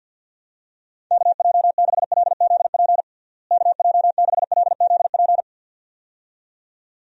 Morse code, heard as a keyed tone, about 50 wpm, 700 Hz. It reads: KJ6LZP KJ6LZP
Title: USA Call Signs - 6 Characters - Repeated 2x - 50wpm